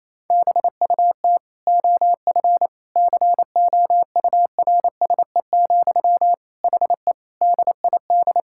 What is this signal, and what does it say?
Morse code, heard as a keyed tone, about 28 wpm, 700 Hz. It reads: BUT OF COURSE, HE DID
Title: Sentences from Top 1600 Words 28wpm